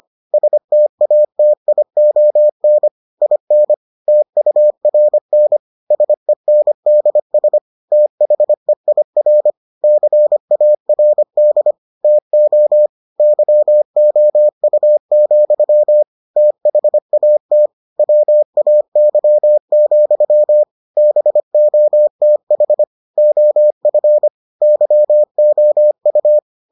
Morse code, heard as a keyed tone, about 25 wpm, 600 Hz. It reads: STATION IN TURN SENDS THEIR CARD TO YOU, THAT WAY, BOTH OF YOU